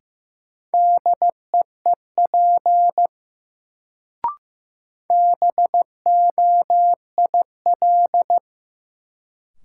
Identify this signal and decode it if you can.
Morse code, heard as a keyed tone, about 15 wpm, 700 Hz. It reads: DEEP BOIL
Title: Sets of 5 Words Encore - New Word Drill - 15wpm